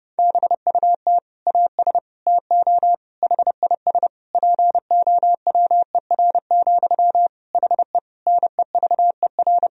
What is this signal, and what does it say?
Morse code, heard as a keyed tone, about 32 wpm, 700 Hz. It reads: BUT AS TO HIS POWER, HE NEVER
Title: Sentences from Top 1000 Words - Review Part 2 - 30wpm